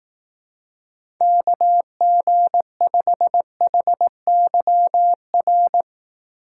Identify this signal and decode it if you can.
Morse code, heard as a keyed tone, about 18 wpm, 700 Hz. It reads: KG5HYR
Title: Speed Racing - USA 2x3 Call Signs 15wpm